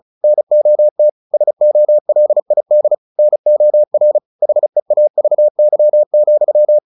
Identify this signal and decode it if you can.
Morse code, heard as a keyed tone, about 34 wpm, 600 Hz. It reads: NOT SOLID NOR HEAVY,
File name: the old man and the sea / the old man and the sea 1 35wpm